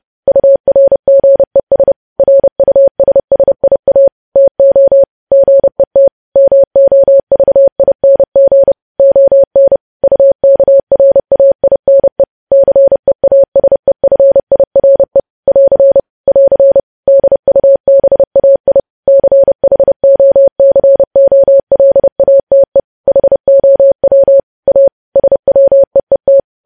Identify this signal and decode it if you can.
Morse code, heard as a keyed tone, about 30 wpm, 550 Hz. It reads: URGES RUSSIA TO GET MOVING ON UKRAINE CEASEFIRE <AR> <AR> DUBAI CHOCOLATE HOW A SWEET
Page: news headlines in morse code at 30 wpm